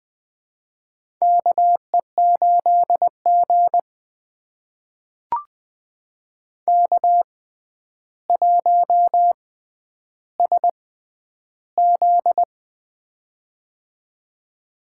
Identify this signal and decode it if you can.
Morse code, heard as a keyed tone, about 20 wpm, spaced as for 10 wpm, 700 Hz. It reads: KE8G K1SZ
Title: USA Call Signs - 4 Characters - 6x Character Spacing - 20wpm